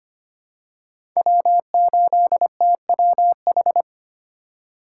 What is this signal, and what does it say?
Morse code, heard as a keyed tone, about 25 wpm, 700 Hz. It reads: W8TWH
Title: USA Call Signs - 2x Character Spacing - 25wpm